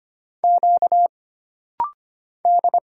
Morse code, heard as a keyed tone, about 25 wpm, 700 Hz. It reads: Q D
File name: Single Character (Letter:Number:Punctuation:Prosign) - Rapid-Fire - 25wpm